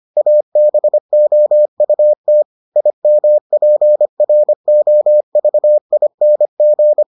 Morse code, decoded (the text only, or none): ABOUT IMPROVING